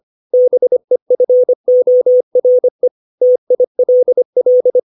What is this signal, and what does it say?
Morse code, heard as a keyed tone, about 25 wpm, 500 Hz. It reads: BEFORE TILL
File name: 632 words / 632 Words 25WPM